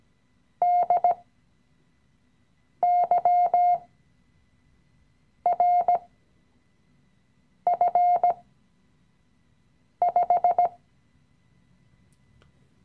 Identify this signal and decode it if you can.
Morse code, heard as a keyed tone, about 17 wpm, spaced as for 5 wpm, 710 Hz. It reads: DYRF5
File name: CW-Online-Kurs / Anfänger / Lektion 017